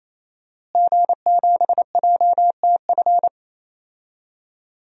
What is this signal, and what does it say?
Morse code, heard as a keyed tone, about 27 wpm, 700 Hz. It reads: G7JTF
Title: DXCC-100 Call Signs - 5 Characters - Repeated 3x - 28wpm